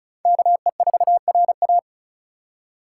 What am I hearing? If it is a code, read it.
Morse code, heard as a keyed tone, about 34 wpm, 700 Hz. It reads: KE4RA
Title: USA Call Signs - 5 Characters - 2x Character Spacing - 35wpm